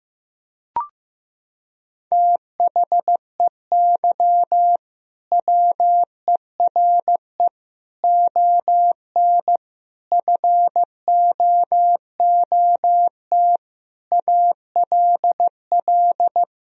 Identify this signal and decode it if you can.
Morse code, heard as a keyed tone, about 15 wpm, 700 Hz. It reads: THEY WERE ON FOOT ALL
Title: Sentences from Top 1000 Words - Review Part 3 - 15wpm